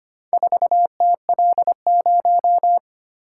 Morse code, heard as a keyed tone, about 25 wpm, 700 Hz. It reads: 4TL0
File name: practice / 25wpm / Four Letter-Number Combinations - 25wpm